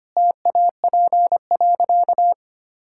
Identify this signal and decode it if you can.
Morse code, heard as a keyed tone, about 25 wpm, 700 Hz. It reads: TAP.
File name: Easy Sentences with Hints - 25wpm